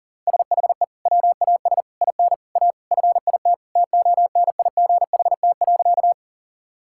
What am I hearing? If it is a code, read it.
Morse code, heard as a keyed tone, about 40 wpm, 700 Hz. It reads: SHE WAS IN A FIT TONIGHT.
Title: Sentences from Top 1000 Words - Review Part 5 - 40wpm